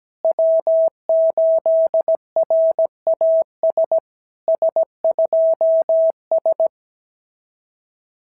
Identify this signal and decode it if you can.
Morse code, heard as a keyed tone, about 17 wpm, 650 Hz. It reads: W8RAS S2S
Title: SOTA QSO Practice 17wpm